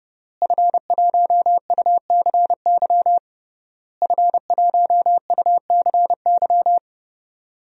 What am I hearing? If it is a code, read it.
Morse code, heard as a keyed tone, about 30 wpm, 700 Hz. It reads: F1UCY F1UCY
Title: DXCC-100 Call Signs - 5 Characters - Repeated 3x - 30wpm